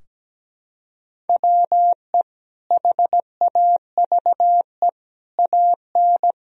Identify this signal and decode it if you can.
Morse code, heard as a keyed tone, about 17 wpm, 700 Hz. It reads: WE HAVE AN
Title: Sentences from Top 1000 Words - Review Part 3 - 17wpm